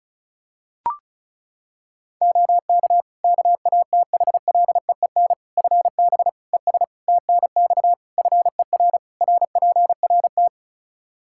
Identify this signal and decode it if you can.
Morse code, heard as a keyed tone, about 36 wpm, 700 Hz. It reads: OK KATHLEEN FB ES TNX FER RPRT
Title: Practice QSO - Exchange 2 - Info - 35wpm